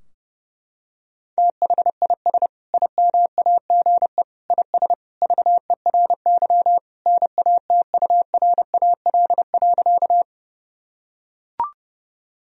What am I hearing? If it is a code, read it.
Morse code, heard as a keyed tone, about 30 wpm, 700 Hz. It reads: THIS IMAGE IS VERY NATURAL.
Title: Sentences from Top 1000 Words - Review Part 4 - 30wpm